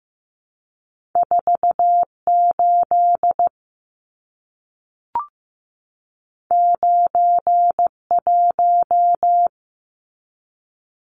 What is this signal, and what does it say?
Morse code, heard as a keyed tone, about 15 wpm, 700 Hz. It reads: 48 91
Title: Numbers - 2-Digit 15wpm